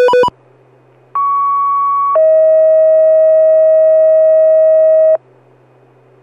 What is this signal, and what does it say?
Two-tone page, heard as a one-second tone, then a three-second tone, about 1120 then 640 Hz. It is heard from 1.1 to 5.2 s.